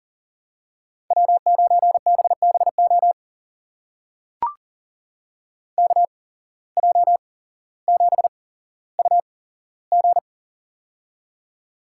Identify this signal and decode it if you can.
Morse code, heard as a keyed tone, about 40 wpm, spaced as for 17 wpm, 700 Hz. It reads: W9BBO KJ7UG
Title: USA Call Signs - 5 Characters - 8x Character Spacing - 40wpm